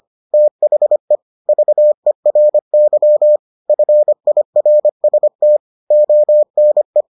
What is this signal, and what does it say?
Morse code, heard as a keyed tone, about 25 wpm, 600 Hz. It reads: THE VERY FIRST ONE